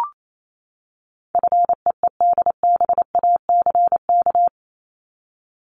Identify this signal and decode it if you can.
Morse code, heard as a keyed tone, about 28 wpm, 700 Hz. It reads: FEEDBACK
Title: Top 1501-1750 Words 28wpm